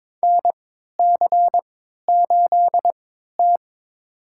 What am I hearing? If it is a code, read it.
Morse code, heard as a keyed tone, about 22 wpm, spaced as for 13 wpm, 700 Hz. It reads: NC8T
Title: USA Call Signs - 4 Characters - 3x Character Spacing - 22wpm